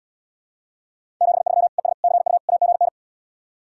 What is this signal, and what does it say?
Morse code, heard as a keyed tone, about 80 wpm, 690 Hz. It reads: 73 ES CU AGN